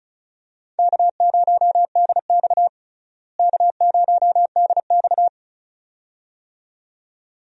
Morse code, heard as a keyed tone, about 34 wpm, 700 Hz. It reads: K0DX K0DX
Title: USA Call Signs - 4 Characters - Repeated 2x - 35wpm